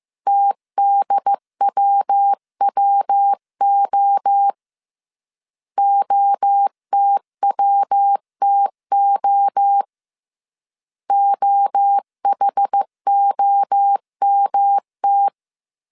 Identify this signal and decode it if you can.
Morse code, heard as a keyed tone, about 15 wpm, 800 Hz. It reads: TDWWO OTWTO OHOMT
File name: CW / DET / 1 3 1500C S A 14wpm 2447